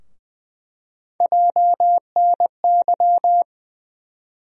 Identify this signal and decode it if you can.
Morse code, heard as a keyed tone, about 20 wpm, 700 Hz. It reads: JNY
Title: USA Call Sign Postfixes - 3 Letters 20wpm